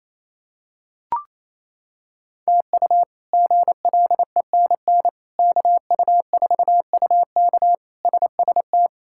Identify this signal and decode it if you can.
Morse code, heard as a keyed tone, about 28 wpm, 700 Hz. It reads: TU GLENN KU4UK SST
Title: K1USN SST QSO Practice - 28wpm